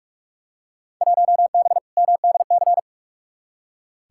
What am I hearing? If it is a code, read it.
Morse code, heard as a keyed tone, about 46 wpm, 700 Hz. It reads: 1B MDC
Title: Field Day Exchanges - 45wpm